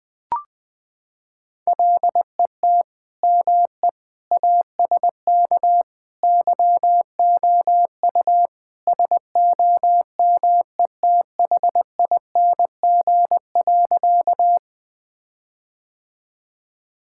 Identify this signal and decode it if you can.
Morse code, heard as a keyed tone, about 20 wpm, 700 Hz. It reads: LET ME ASK YOU SOMETHING.
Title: Sentences with 4-Word Phrases - 20wpm